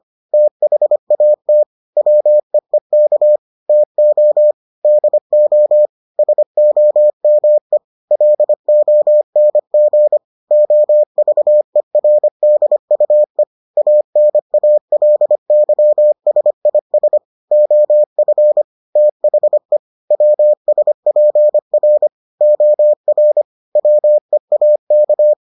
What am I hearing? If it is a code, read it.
Morse code, heard as a keyed tone, about 25 wpm, 600 Hz. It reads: THAT WEEK TO DO SOME LONG OVERDUE ANALYSIS OF THE WSPR OR WEAK